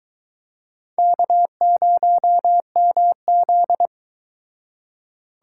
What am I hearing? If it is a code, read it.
Morse code, heard as a keyed tone, about 23 wpm, 700 Hz. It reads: K0MZ